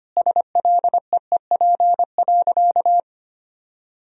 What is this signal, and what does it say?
Morse code, heard as a keyed tone, about 25 wpm, 700 Hz. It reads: SLEEP.
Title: Sentences from Top 1000 Words - Review Part 2 - 25wpm